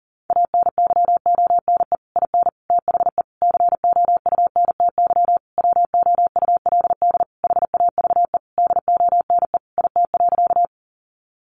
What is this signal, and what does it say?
Morse code, heard as a keyed tone, about 37 wpm, 700 Hz. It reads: ANYONE IN THE COUNTY WOULD HAVE DONE IT.
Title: Sentences from Top 1200 Words 40wpm